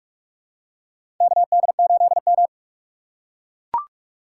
Morse code, heard as a keyed tone, about 43 wpm, 690 Hz. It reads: KD8M
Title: USA Call Signs - 4 Characters - 3x Character Spacing - 45wpm